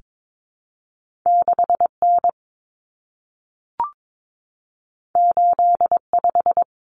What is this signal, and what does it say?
Morse code, heard as a keyed tone, about 22 wpm, 700 Hz. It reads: 6N 85
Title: Common Cut Numbers - 2-Digit 22wpm